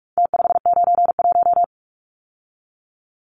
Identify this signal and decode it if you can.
Morse code, heard as a keyed tone, about 44 wpm, 700 Hz. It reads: T591
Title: Numbers - Contest Style - 4-Digit 45wpm